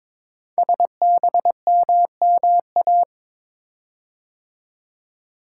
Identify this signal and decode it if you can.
Morse code, heard as a keyed tone, about 22 wpm, 700 Hz. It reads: SBMMA
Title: Code Groups - 5 Letters - 22wpm